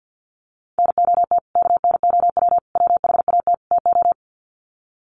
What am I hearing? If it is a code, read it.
Morse code, heard as a keyed tone, about 50 wpm, 700 Hz. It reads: NOT KNOW WHAT TO
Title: Sets of 4 Words 50wpm